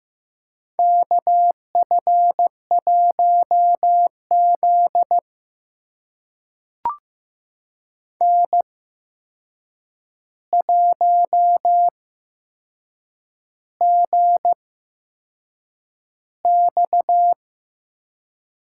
Morse code, heard as a keyed tone, about 15 wpm, spaced as for 6 wpm, 700 Hz. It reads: KF1Z N1GX
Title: USA Call Signs - 4 Characters - 8x Character Spacing - 15wpm